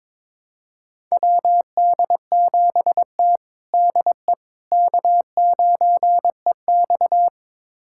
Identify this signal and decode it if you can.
Morse code, heard as a keyed tone, about 22 wpm, 700 Hz. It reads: WD7T DE K9EX
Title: USA Callsign DE Callsign - 4 Characters - 22wpm